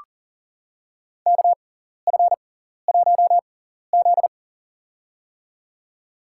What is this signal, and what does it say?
Morse code, heard as a keyed tone, about 40 wpm, spaced as for 14 wpm, 700 Hz. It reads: KF1Z
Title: USA Call Signs - 4 Characters - 6x Character Spacing - 40wpm